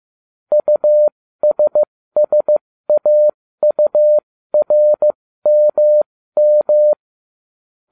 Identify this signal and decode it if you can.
Morse code, heard as a keyed tone, about 15 wpm, spaced as for 13 wpm, 600 Hz. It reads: USSAURMM